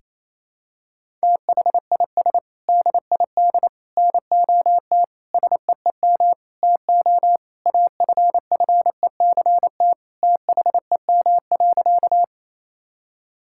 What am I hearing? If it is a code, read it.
Morse code, heard as a keyed tone, about 28 wpm, 700 Hz. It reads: THIS DID NOT SEEM TO AFFECT THEM.
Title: Sentences from Top 1000 Words - Review Part 6 - 28wpm